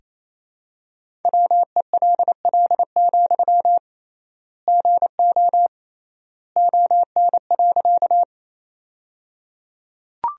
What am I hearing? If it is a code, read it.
Morse code, heard as a keyed tone, about 28 wpm, 700 Hz. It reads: WELL, GO ON.
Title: 3x Word-Spacing - Sentences from Top 200 Words - 28wpm